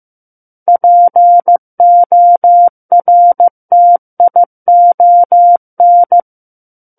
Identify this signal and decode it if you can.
Morse code, heard as a keyed tone, about 15 wpm, 700 Hz. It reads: PORTION
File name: Chapter 6 - 15 WPM